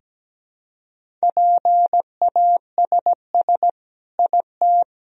Morse code, heard as a keyed tone, about 17 wpm, 700 Hz. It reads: PASS IT
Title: Sentences from Top 1000 Words - Review Part 1 - 17wpm